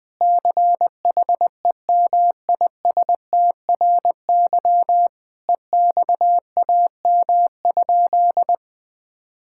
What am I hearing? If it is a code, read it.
Morse code, heard as a keyed tone, about 20 wpm, 700 Hz. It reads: CHEMISTRY EXAM?
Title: Sentences from Top 1000 Words - Review Part 6 - 20wpm